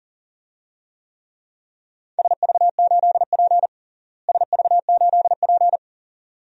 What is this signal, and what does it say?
Morse code, heard as a keyed tone, about 40 wpm, 700 Hz. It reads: SV8P SV8P
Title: DXCC-100 Call Signs - 4 Characters - Repeated 2x - 40wpm